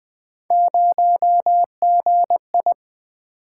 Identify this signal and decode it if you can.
Morse code, heard as a keyed tone, about 20 wpm, 700 Hz. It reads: 0GI
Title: Three Letter-Number Combinations - 20wpm